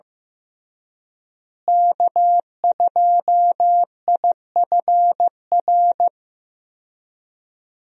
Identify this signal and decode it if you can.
Morse code, heard as a keyed tone, about 15 wpm, 700 Hz. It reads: K2IFR
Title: USA Call Signs - 5 Characters - Repeated 3x - 15wpm